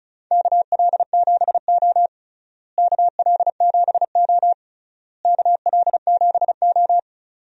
Morse code, heard as a keyed tone, about 36 wpm, 700 Hz. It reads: KL7O KL7O KL7O
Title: DXCC-100 Call Signs - 4 Characters - Repeated 3x - 35wpm